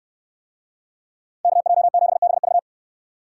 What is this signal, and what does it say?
Morse code, heard as a keyed tone, about 68 wpm, 700 Hz. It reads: KJ7BV